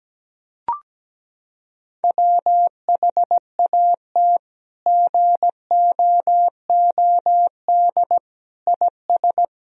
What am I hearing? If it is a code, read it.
Morse code, heard as a keyed tone, about 17 wpm, 700 Hz. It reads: WHAT GOOD IS
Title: Sentences from Top 1000 Words - Review Part 3 - 17wpm